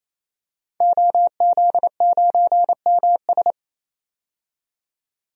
Morse code, heard as a keyed tone, about 28 wpm, 700 Hz. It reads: OZ9MS
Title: DXCC-100 Call Signs - 5 Characters - 28wpm